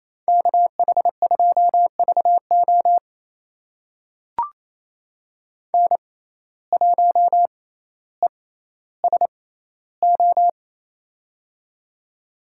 Morse code, heard as a keyed tone, about 28 wpm, spaced as for 14 wpm, 700 Hz. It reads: KH2VO N1ESO
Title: USA Call Signs - 5 Characters - 6x Character Spacing - 28wpm